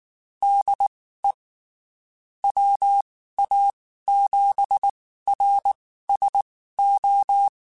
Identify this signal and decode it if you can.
Morse code, heard as a keyed tone, about 19 wpm, spaced as for 14 wpm, 800 Hz. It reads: DE WA7RSO
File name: (spaced) / QSOs Spaced-mp3s / kc7haa @18 sp